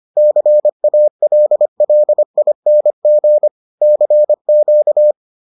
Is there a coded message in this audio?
Yes: Morse code from 0.2 to 5.1 s, heard as a keyed tone, about 25 wpm, 600 Hz. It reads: CALLING CQ